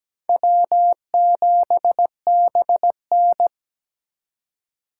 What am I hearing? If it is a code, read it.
Morse code, heard as a keyed tone, about 17 wpm, 700 Hz. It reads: W7BN
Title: USA Call Signs - 4 Characters - 3x Character Spacing - 17wpm